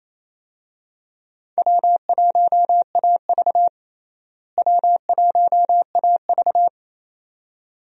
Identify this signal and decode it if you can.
Morse code, heard as a keyed tone, about 28 wpm, 700 Hz. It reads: W1AV W1AV
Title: USA Call Signs - 4 Characters - Repeated 2x - 28wpm